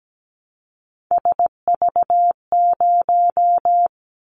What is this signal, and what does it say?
Morse code, heard as a keyed tone, about 17 wpm, 700 Hz. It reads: SV0
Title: Callsign Prefixes for Top 50 DXCC - 17wpm